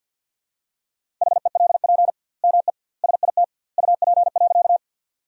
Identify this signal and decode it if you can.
Morse code, heard as a keyed tone, about 52 wpm, 700 Hz. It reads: HELP ME SIT UP.
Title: 2x Word-Spacing - Sentences from Top 500 Words - 50wpm